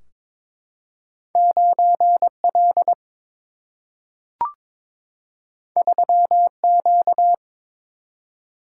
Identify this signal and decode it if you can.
Morse code, heard as a keyed tone, about 22 wpm, 700 Hz. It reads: 9L 3Q
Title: Single-Letter Single-Number 22wpm